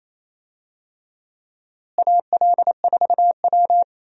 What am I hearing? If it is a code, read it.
Morse code, heard as a keyed tone, about 28 wpm, 700 Hz. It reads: AL4W
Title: USA Call Signs - 4 Characters - Repeated 2x - 28wpm